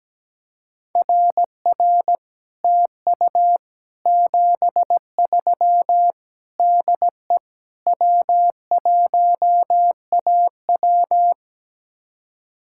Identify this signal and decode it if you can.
Morse code, heard as a keyed tone, about 17 wpm, 700 Hz. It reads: RR TU 73 DE W1AW
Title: Speed-Racing Common QSO Phrases Encore - 17wpm